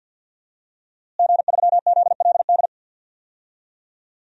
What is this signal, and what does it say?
Morse code, heard as a keyed tone, about 52 wpm, 700 Hz. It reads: G3ZLD
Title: DXCC-100 Call Signs - 5 Characters - Repeated 3x - 50wpm